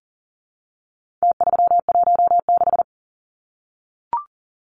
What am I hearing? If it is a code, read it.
Morse code, heard as a keyed tone, about 40 wpm, 700 Hz. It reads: T316
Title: Numbers - Contest Style - 4-Digit 40wpm